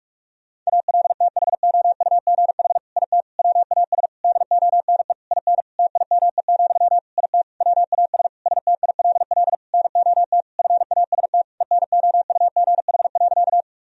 Morse code, heard as a keyed tone, about 46 wpm, 700 Hz. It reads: ALTHOUGH IT WAS DONE IN TIME, IT WAS STILL NOT FAST ENOUGH.
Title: Sentences from Top 1000 Words - Review Part 3 - 45wpm